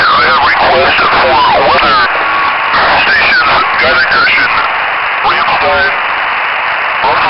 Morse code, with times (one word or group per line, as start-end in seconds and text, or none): none